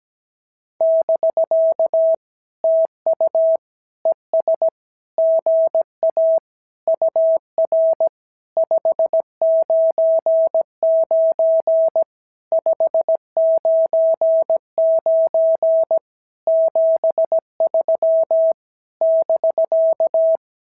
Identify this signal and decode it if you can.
Morse code, heard as a keyed tone, about 17 wpm, 650 Hz. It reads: <BK> TU ES GA UR 599 599 73 <BK>
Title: SOTA QSO Practice 17wpm